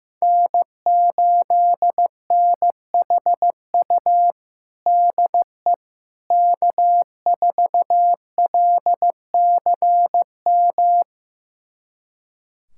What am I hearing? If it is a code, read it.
Morse code, heard as a keyed tone, about 15 wpm, 700 Hz. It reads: N8NHU DE K4LCM
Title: USA Callsign DE Callsign - 5 Characters - 15wpm